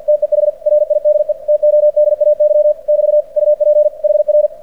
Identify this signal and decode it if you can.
Morse code, heard as a keyed tone, about 48 wpm, 600 Hz. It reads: TEH RIDE TODAY <BT> UP HR
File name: archived / CRSnet